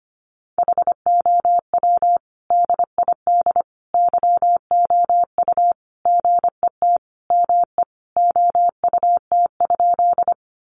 Morse code, heard as a keyed tone, about 25 wpm, 700 Hz. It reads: HOW DID YOU GET ME OUT?
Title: Sentences from Top 200 Words 25wpm